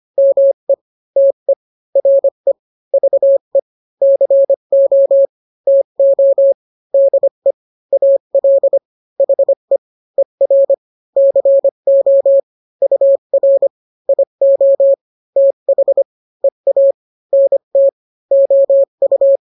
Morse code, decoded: ME TE RE VE CO TO DE AL HE ER CO UR IO TH EA NT OU